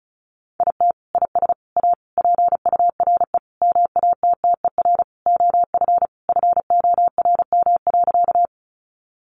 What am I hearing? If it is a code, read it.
Morse code, heard as a keyed tone, about 34 wpm, 700 Hz. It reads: IT IS A PURE MATTER OF FORM.
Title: Sentences from Top 1500 Words 35wpm